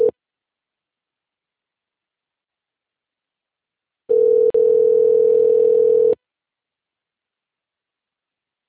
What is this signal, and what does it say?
Ringback tone (North America), 440 + 480 Hz.